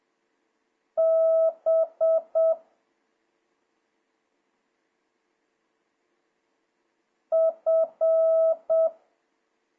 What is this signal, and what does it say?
Morse code, heard as a keyed tone, about 7 wpm, 650 Hz. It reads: B F
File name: jh8cbh / 2026cw